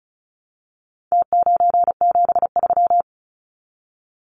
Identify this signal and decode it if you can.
Morse code, heard as a keyed tone, about 34 wpm, 700 Hz. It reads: T973
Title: Numbers - Contest Style - 4-Digit 35wpm